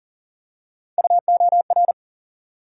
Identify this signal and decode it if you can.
Morse code, heard as a keyed tone, about 40 wpm, 700 Hz. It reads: UOR
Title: USA Call Sign Postfixes - 3 Letters 40wpm